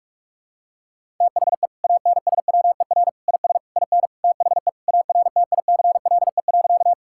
Morse code, heard as a keyed tone, about 41 wpm, 700 Hz. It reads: THE ANSWER IS IN THE ARTICLE.
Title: Sentences from Top 1000 Words - Review Part 5 - 45wpm